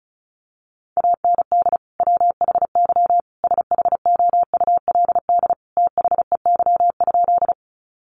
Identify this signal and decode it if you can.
Morse code, heard as a keyed tone, about 34 wpm, 700 Hz. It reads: AND WHY SHOULD THEY?
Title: Sentences from Top 300 Words 35wpm